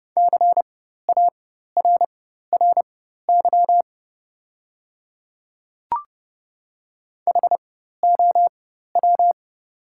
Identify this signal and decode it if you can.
Morse code, heard as a keyed tone, about 30 wpm, spaced as for 14 wpm, 700 Hz. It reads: CARRY HOW
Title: 4x Character Spacing - Top 300 Words - 30wpm